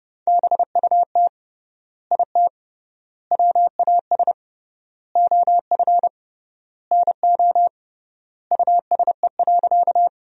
Morse code, heard as a keyed tone, about 32 wpm, 700 Hz. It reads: BUT IT WAS OF NO USE.
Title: 3x Word-Spacing - Sentences from Top 100 Words - 30wpm